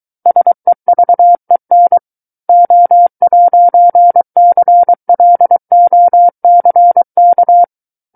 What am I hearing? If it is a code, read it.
Morse code, heard as a keyed tone, about 23 wpm, 700 Hz. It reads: SEVEN O'CLOCK
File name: Book One / 14-Chapter 14 - 23 WPM